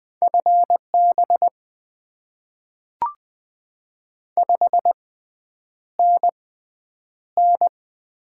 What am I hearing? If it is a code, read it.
Morse code, heard as a keyed tone, about 20 wpm, spaced as for 9 wpm, 700 Hz. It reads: FB 5NN
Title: Top 20 QSO Elements - 6x Character Spacing - 20wpm